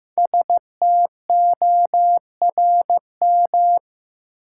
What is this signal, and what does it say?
Morse code, heard as a keyed tone, about 15 wpm, 700 Hz. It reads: STORM
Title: Sentences from Top 1500 Words - New Word Drill - 15wpm